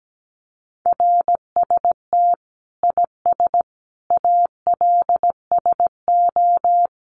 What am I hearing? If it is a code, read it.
Morse code, heard as a keyed tone, about 17 wpm, 700 Hz. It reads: RST IS ALSO
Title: Common QSO Phrases 17wpm